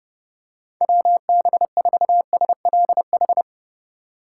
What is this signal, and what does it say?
Morse code, heard as a keyed tone, about 30 wpm, 700 Hz. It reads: WB4SLH